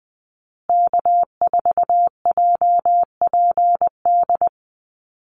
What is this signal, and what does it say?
Morse code, heard as a keyed tone, about 20 wpm, 700 Hz. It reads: K4JPD
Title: Call Signs Repeated 2x - 20wpm